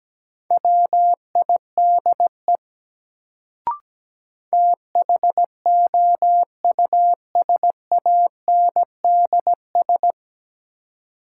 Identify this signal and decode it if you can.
Morse code, heard as a keyed tone, about 17 wpm, 700 Hz. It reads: WIDE THOUSANDS
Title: Sets of 4 Words - New Word Drill - 17wpm